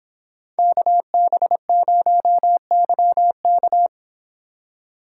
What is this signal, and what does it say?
Morse code, heard as a keyed tone, about 30 wpm, 700 Hz. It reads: KB0YK